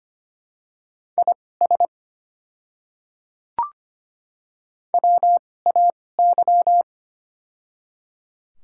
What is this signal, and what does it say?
Morse code, heard as a keyed tone, about 25 wpm, spaced as for 18 wpm, 700 Hz. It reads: IS WAY
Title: 2x Character Spacing - Top 100 Words - 25wpm